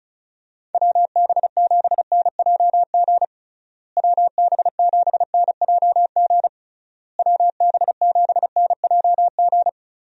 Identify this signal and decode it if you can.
Morse code, heard as a keyed tone, about 34 wpm, 700 Hz. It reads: WB7NJG WB7NJG WB7NJG
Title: USA Call Signs - 6 Characters - Repeated 3x - 35wpm